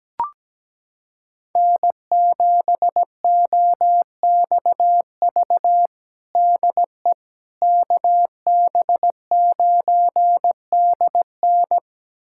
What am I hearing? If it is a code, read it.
Morse code, heard as a keyed tone, about 17 wpm, 700 Hz. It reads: N7OXV DE KB9DN
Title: USA Callsign DE Callsign - 5 Characters - 17wpm